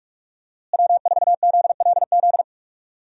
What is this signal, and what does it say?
Morse code, heard as a keyed tone, about 46 wpm, 700 Hz. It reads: W4ZLZ
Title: USA Call Signs - 5 Characters - Repeated 3x - 45wpm